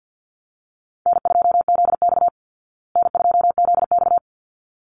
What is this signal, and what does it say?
Morse code, heard as a keyed tone, about 49 wpm, 700 Hz. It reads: N2ZX N2ZX
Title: Call Signs Repeated 3x - 50wpm